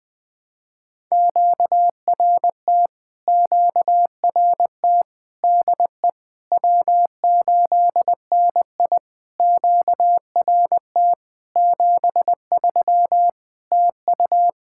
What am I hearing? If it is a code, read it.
Morse code, heard as a keyed tone, about 20 wpm, 700 Hz. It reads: QRT QRT DE W8NI QRT 73 TU
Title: POTA QSO Practice - 20wpm